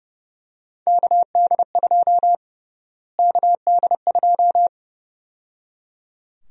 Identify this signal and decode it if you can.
Morse code, heard as a keyed tone, about 32 wpm, 700 Hz. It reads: KD2 KD2
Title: USA Call Sign Prefixes - Repeated 2x - 30wpm